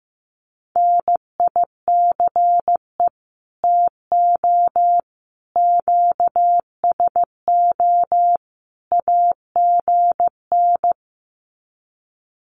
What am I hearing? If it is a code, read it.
Morse code, heard as a keyed tone, about 15 wpm, 700 Hz. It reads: NICE TO QSO AGN
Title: Common QSO Phrases 15wpm